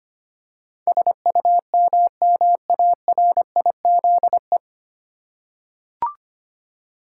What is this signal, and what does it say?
Morse code, heard as a keyed tone, about 25 wpm, 700 Hz. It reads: SUMMARIZE
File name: Nine Letter Words - 25wpm